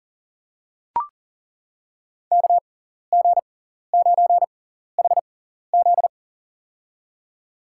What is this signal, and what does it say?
Morse code, heard as a keyed tone, about 40 wpm, spaced as for 14 wpm, 700 Hz. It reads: KG9HZ
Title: USA Call Signs - 6x Character Spacing - 40wpm